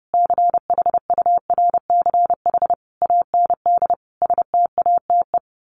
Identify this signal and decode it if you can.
Morse code, heard as a keyed tone, about 30 wpm, 700 Hz. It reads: CHURCH AND STATE
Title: Sets of 5 Words 30wpm